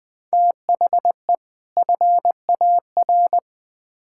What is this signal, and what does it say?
Morse code, heard as a keyed tone, about 20 wpm, 700 Hz. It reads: THE FAR
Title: Build A Phrase - 5-Word Phrases - 20wpm